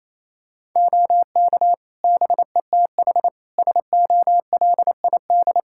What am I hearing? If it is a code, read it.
Morse code, heard as a keyed tone, about 28 wpm, 700 Hz. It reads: OK BETH SOLID
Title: Practice QSO - Exchange 3 - More Info - 28wpm